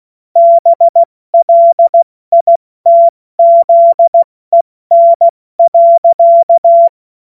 Morse code, heard as a keyed tone, about 16 wpm, 680 Hz. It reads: BLITZEN.
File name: Morsekurs / Witze / Witz 14. März 2023 70 F80